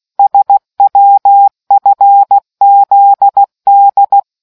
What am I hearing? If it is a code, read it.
Morse code, heard as a keyed tone, about 16 wpm, 800 Hz. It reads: SWFZD